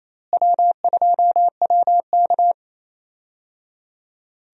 Morse code, heard as a keyed tone, about 28 wpm, 700 Hz. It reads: W2WK
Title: USA Call Signs - 4 Characters - Repeated 3x - 28wpm